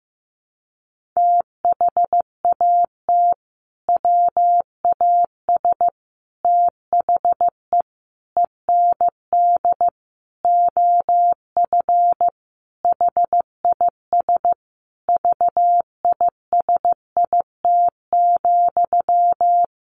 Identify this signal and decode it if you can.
Morse code, heard as a keyed tone, about 15 wpm, 700 Hz. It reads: THAT WAS THE END OF HIS VISIT,
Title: Sentences from Top 1400 Words 15wpm